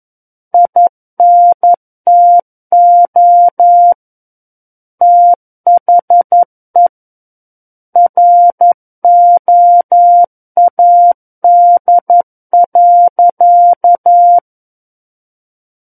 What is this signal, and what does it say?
Morse code, heard as a keyed tone, about 11 wpm, 700 Hz. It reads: INTO THE ROAD.
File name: 11 WPM